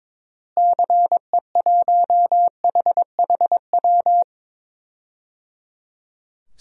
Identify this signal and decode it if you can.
Morse code, heard as a keyed tone, about 22 wpm, 700 Hz. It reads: CE1HHW
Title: DXCC-100 Call Signs - 6 Characters - Repeated 3x - 22wpm